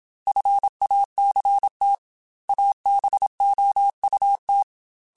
Morse code, heard as a keyed tone, about 27 wpm, 800 Hz. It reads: FACT ABOUT
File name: Technical Subjects / Technical Subjects-mp3s / AND Gate @25